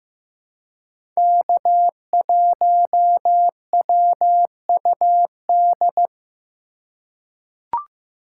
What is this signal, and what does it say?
Morse code, heard as a keyed tone, about 16 wpm, 700 Hz. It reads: K1WUD
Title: USA Call Signs - 2x Character Spacing - 15wpm